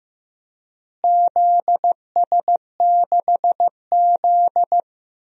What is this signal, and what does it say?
Morse code, heard as a keyed tone, about 15 wpm, 700 Hz. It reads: ZS6Z
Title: DXCC-100 Call Signs - 4 Characters - Repeated 2x - 15wpm